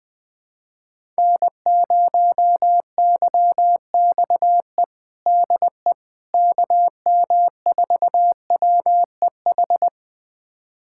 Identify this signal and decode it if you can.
Morse code, heard as a keyed tone, about 20 wpm, 700 Hz. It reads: N0YXE DE KM4WEH
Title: USA Callsign DE Callsign 20wpm